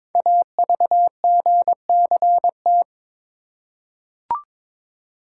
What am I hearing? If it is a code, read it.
Morse code, heard as a keyed tone, about 23 wpm, 700 Hz. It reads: AVGCT
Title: Code Groups - 5 Letter:Numbers - 22wpm